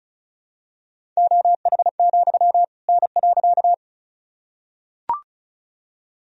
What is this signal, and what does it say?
Morse code, heard as a keyed tone, about 36 wpm, 700 Hz. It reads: OH, N.
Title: Sentences from Top 500 Words and Abbreviations 35wpm